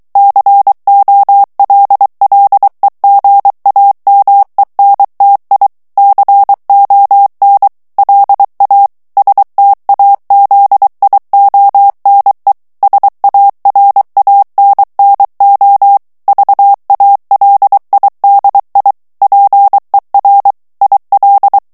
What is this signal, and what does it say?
Morse code, heard as a keyed tone, about 23 wpm, 800 Hz. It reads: COLLEGAMENTI CON LA STAZIONE SARANNO VALIDI PER IL
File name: morse mp3 / ITALIEN texte clair 1 vitesse 22 mots minute